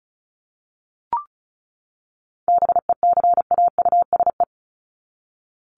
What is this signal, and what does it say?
Morse code, heard as a keyed tone, about 34 wpm, 700 Hz. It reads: BECAUSE
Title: Top 501-750 Words 35wpm